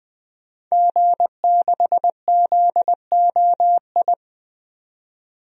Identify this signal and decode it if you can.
Morse code, heard as a keyed tone, about 20 wpm, 700 Hz. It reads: G6ZOI
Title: DXCC-100 Call Signs - 5 Characters - 20wpm